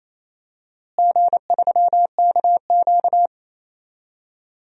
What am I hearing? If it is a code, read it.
Morse code, heard as a keyed tone, about 28 wpm, 700 Hz. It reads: G3KQ
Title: DXCC-100 Call Signs - 4 Characters - Repeated 2x - 28wpm